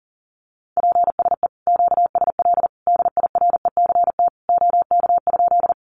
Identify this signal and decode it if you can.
Morse code, heard as a keyed tone, about 40 wpm, 700 Hz. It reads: PSE QSL DIRECT OK?
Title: Common QSO Phrases 40wpm